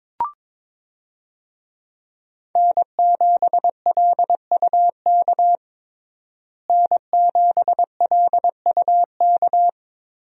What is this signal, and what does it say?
Morse code, heard as a keyed tone, about 22 wpm, 700 Hz. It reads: N7LUK N7LUK
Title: USA Call Signs - 5 Characters - Repeated 3x - 22wpm